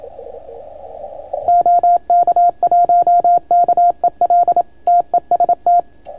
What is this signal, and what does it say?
Morse code, heard as a keyed tone, about 27 wpm, 700 Hz, with radio noise. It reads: OK1KEL TEST